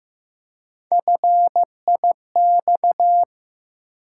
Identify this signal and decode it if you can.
Morse code, heard as a keyed tone, about 15 wpm, 700 Hz. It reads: FIX
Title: Three Letter Words - 15wpm